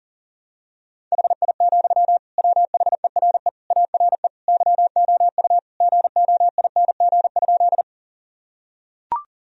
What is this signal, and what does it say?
Morse code, heard as a keyed tone, about 37 wpm, 700 Hz. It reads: HI, WHERE ARE YOU GOING?
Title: Sentences from Top 1000 Words - Review Part 3 - 40wpm